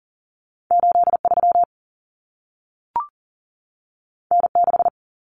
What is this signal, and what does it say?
Morse code, heard as a keyed tone, about 40 wpm, 700 Hz. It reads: 8 3 N 6
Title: Common Cut Numbers - 2-Digit 40wpm